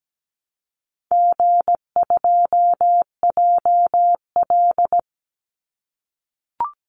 Morse code, heard as a keyed tone, about 17 wpm, 700 Hz. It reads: G2JL
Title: Easy Call Signs - 17wpm